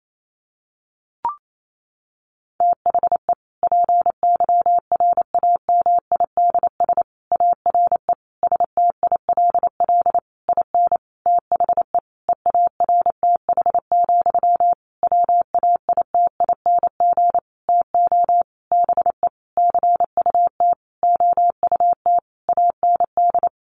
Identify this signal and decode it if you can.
Morse code, heard as a keyed tone, about 28 wpm, 700 Hz. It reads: THE PYRAMIDS ARE STILL IN THE EARTH, WAITING TO BE CUT OUT AND
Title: A Sound Of Thunder 28wpm